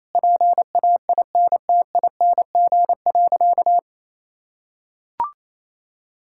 Morse code, heard as a keyed tone, about 28 wpm, 700 Hz. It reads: PAINTING.
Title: Sentences from Top 1000 Words - Review Part 6 - 28wpm